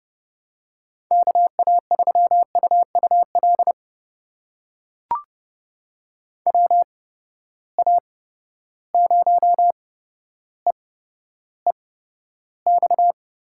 Morse code, heard as a keyed tone, about 32 wpm, spaced as for 13 wpm, 700 Hz. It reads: KA3UUL WA0EEX
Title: USA Call Signs - 6 Characters - 8x Character Spacing - 30wpm